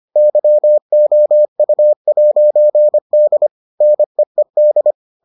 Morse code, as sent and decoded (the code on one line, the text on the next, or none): -.-- --- ..- .----. -.. / -. . -..
YOU'D NEED